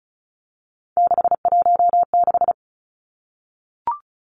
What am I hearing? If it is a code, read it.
Morse code, heard as a keyed tone, about 36 wpm, 700 Hz. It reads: 616
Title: Three Challenging Letter-Number Combinations 35wpm